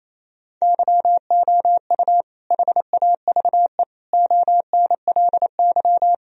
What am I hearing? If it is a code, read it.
Morse code, heard as a keyed tone, about 28 wpm, 700 Hz. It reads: YOU HAVE ONLY
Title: Sentences from Top 1000 Words - Review Part 1 - 28wpm